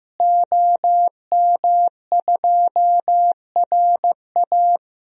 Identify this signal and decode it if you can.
Morse code, heard as a keyed tone, about 15 wpm, 700 Hz. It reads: OM2RA
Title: DXCC-100 Call Signs - 5 Characters - Repeated 2x - 15wpm